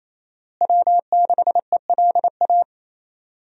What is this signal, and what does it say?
Morse code, heard as a keyed tone, about 28 wpm, 700 Hz. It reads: W6ELA
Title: USA Call Signs - 5 Characters - Repeated 3x - 28wpm